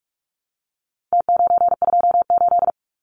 Morse code, heard as a keyed tone, about 44 wpm, 700 Hz. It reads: T928
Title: Numbers - Contest Style - 4-Digit 45wpm